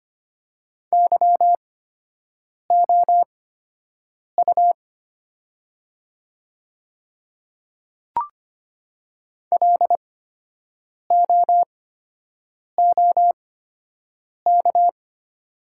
Morse code, heard as a keyed tone, about 25 wpm, spaced as for 7 wpm, 700 Hz. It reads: YOU LOOK
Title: 8x Character Spacing - Top 200 Words - 25wpm